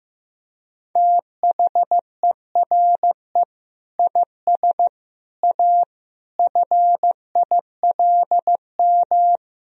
Morse code, heard as a keyed tone, about 15 wpm, 700 Hz. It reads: THERE IS A FILM
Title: Sentences from Top 1000 Words - Review Part 3 - 15wpm